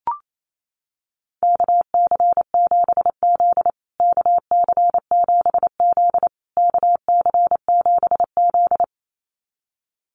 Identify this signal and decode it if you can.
Morse code, heard as a keyed tone, about 28 wpm, 700 Hz. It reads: KC7Z KC7Z KC7Z
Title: USA Call Signs Repeated 3x 28wpm